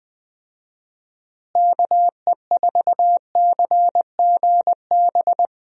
Morse code, heard as a keyed tone, about 21 wpm, 700 Hz. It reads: KE4CGB
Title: USA Call Signs - 6 Characters - Repeated 3x - 20wpm